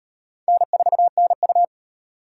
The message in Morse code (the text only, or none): N4NU